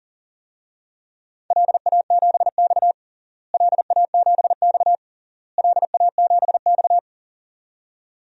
Morse code, heard as a keyed tone, about 40 wpm, 700 Hz. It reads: LA7X LA7X LA7X
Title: DXCC-100 Call Signs - 4 Characters - Repeated 3x - 40wpm